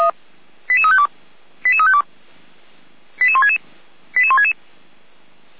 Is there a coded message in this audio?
Yes: five-tone selective call (ZVEI) from 0.7 to 4.5 s, heard as quick runs of five tones, tones of about 70 ms. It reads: 88352 88352 88166 88166